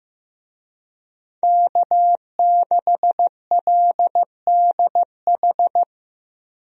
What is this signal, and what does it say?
Morse code, heard as a keyed tone, about 15 wpm, 700 Hz. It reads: K6LDH
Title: USA Call Signs - 5 Characters - Repeated 2x - 15wpm